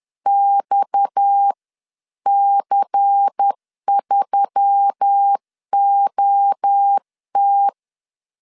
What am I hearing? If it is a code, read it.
Morse code, heard as a keyed tone, about 10 wpm, spaced as for 9 wpm, 800 Hz. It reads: XC3OT
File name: CW / DET / 1 6 1500C S C 10wpm 4252